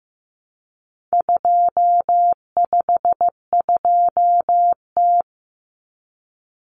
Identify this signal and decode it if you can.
Morse code, heard as a keyed tone, about 15 wpm, 700 Hz. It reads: 252T
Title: Common Cut Numbers - 4-Digit 15wpm